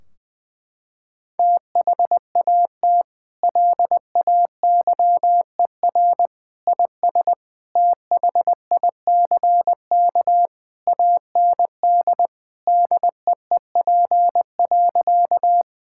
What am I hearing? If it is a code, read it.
Morse code, heard as a keyed tone, about 20 wpm, 700 Hz. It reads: THAT LAYER IS THICK AND DEEP.
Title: Sentences from Top 1600 Words 20wpm